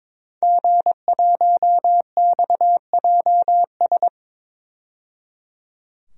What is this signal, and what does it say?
Morse code, heard as a keyed tone, about 22 wpm, 700 Hz. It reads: G1XJS
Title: DXCC-100 Call Signs - 5 Characters - 22wpm